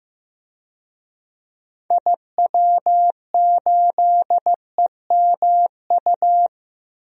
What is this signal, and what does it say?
Morse code, heard as a keyed tone, about 16 wpm, 700 Hz. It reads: IW8EMU
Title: DXCC-100 Call Signs - 6 Characters - 15wpm